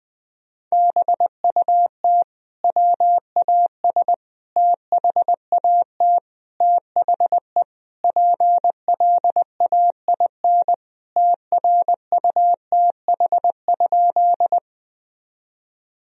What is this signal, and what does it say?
Morse code, heard as a keyed tone, about 20 wpm, 700 Hz. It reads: BUT WAS THAT THE PLAIN TRUTH?
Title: Sentences from Top 1000 Words - Review Part 2 - 20wpm